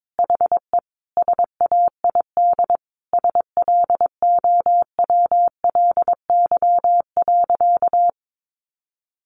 Morse code, heard as a keyed tone, about 22 wpm, 700 Hz. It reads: HE SAID SLOWLY.
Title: Sentences from Top 1100 Words 22wpm